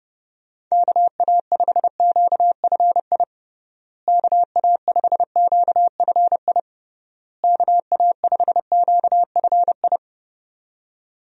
Morse code, heard as a keyed tone, about 30 wpm, 700 Hz. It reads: KA5QFI KA5QFI KA5QFI
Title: USA Call Signs - 6 Characters - Repeated 3x - 30wpm